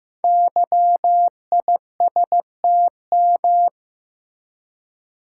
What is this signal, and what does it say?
Morse code, heard as a keyed tone, about 15 wpm, 700 Hz. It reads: YISTM
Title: Code Groups - 5 Letters - 15wpm